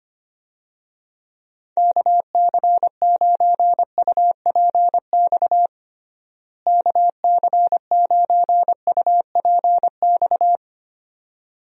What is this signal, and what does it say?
Morse code, heard as a keyed tone, about 25 wpm, 700 Hz. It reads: KC9UPX KC9UPX
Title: USA Call Signs - 6 Characters - Repeated 2x - 25wpm